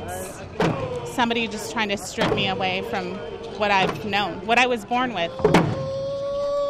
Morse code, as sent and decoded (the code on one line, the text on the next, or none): none